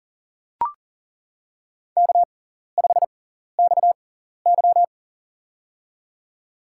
Morse code, heard as a keyed tone, about 40 wpm, spaced as for 14 wpm, 700 Hz. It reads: K5XY
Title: USA Call Signs - 4 Characters - 6x Character Spacing - 40wpm